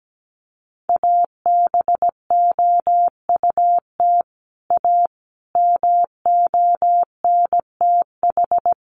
Morse code, heard as a keyed tone, about 17 wpm, 700 Hz. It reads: ABOUT A MONTH
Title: Sentences from Top 800 Words 17wpm